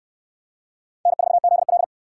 Morse code, heard as a keyed tone, about 68 wpm, 700 Hz. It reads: N4ZL